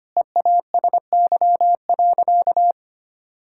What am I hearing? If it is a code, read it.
Morse code, heard as a keyed tone, about 25 wpm, 700 Hz. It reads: EASY.
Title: Sentences from Top 1000 Words - Review Part 6 - 25wpm